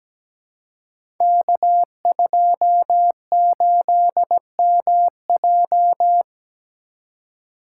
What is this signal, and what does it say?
Morse code, heard as a keyed tone, about 17 wpm, 700 Hz. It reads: K28MJ
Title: Code Groups - 5 Letter:Numbers - 17wpm